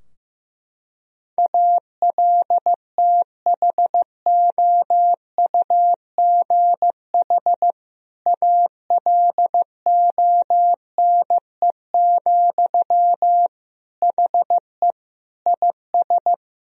Morse code, heard as a keyed tone, about 15 wpm, 700 Hz. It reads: ALTHOUGH ALONE, HE IS
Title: Sentences from Top 1000 Words - Review Part 3 - 15wpm